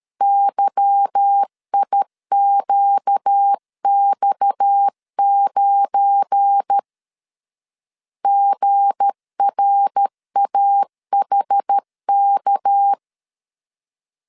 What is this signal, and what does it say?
Morse code, heard as a keyed tone, about 13 wpm, 800 Hz. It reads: YIQX9 GRAHK
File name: LEZIONI / 8 YJ45 / 8 YJ4 12wpm 2151